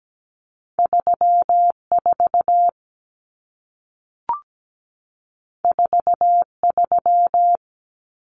Morse code, heard as a keyed tone, about 17 wpm, 700 Hz. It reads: 34 43
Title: Common Cut Numbers - 2-Digit 17wpm